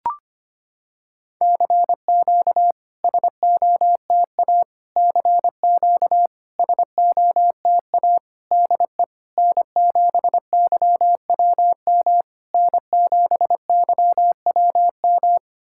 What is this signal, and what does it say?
Morse code, heard as a keyed tone, about 25 wpm, 700 Hz. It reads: CQ SOTA CQ SOTA DE N7YWM N7YWM
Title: SOTA QSO Practice 25wpm